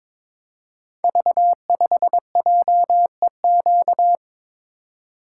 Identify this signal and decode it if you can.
Morse code, heard as a keyed tone, about 22 wpm, 690 Hz. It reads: V5JEQ
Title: Code Groups - 5 Letter:Numbers - 22wpm